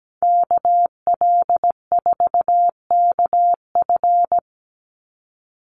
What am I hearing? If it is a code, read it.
Morse code, heard as a keyed tone, about 17 wpm, 700 Hz. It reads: KL4KF
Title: USA Call Signs 17wpm